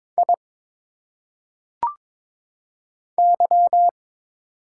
Morse code, heard as a keyed tone, about 22 wpm, 700 Hz. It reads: I Y
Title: Single Letters - Phonetic Pronunciation 22wpm